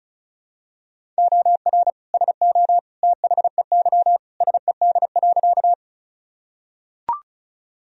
Morse code, heard as a keyed tone, about 34 wpm, 700 Hz. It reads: OR SO THEY SED.
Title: Sentences from Top 100 Words and Abbreviations 35wpm